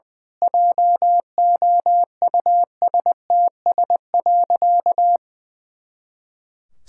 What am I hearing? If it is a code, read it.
Morse code, heard as a keyed tone, about 21 wpm, 700 Hz. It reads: JOUSTS.
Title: Pangram Challenge v2 - 20wpm